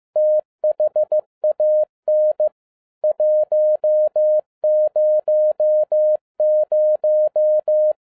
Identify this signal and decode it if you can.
Morse code, heard as a keyed tone, about 15 wpm, 610 Hz. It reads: THAN 100
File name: EXCITING QRP CW OPERATION 15wpm